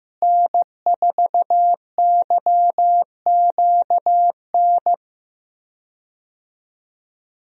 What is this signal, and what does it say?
Morse code, heard as a keyed tone, about 15 wpm, 700 Hz. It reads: N4YQN
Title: USA Call Signs - 5 Characters - Repeated 3x - 15wpm